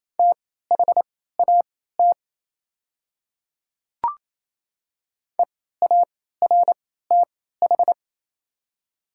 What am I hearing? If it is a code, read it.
Morse code, heard as a keyed tone, about 28 wpm, spaced as for 16 wpm, 700 Hz. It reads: THAT EARTH
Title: 3x Character Spacing - Top 500 Words - 28wpm